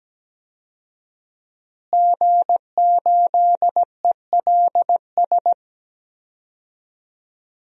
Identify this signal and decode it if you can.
Morse code, heard as a keyed tone, about 17 wpm, 700 Hz. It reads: G8ELS
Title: DXCC-100 Call Signs - 5 Characters - Repeated 2x - 17wpm